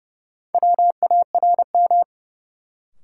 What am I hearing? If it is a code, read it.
Morse code, heard as a keyed tone, about 31 wpm, 700 Hz. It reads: WARM
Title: Sentences from Top 500 Words - New Word Drill - 30wpm